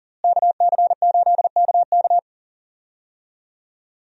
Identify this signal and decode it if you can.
Morse code, heard as a keyed tone, about 40 wpm, 700 Hz. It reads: KC8KK
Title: USA Call Signs - 5 Characters - Repeated 2x - 40wpm